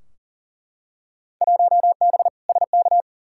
Morse code, heard as a keyed tone, about 40 wpm, 700 Hz. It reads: 1B SK
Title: Field Day Exchanges Encore - 40wpm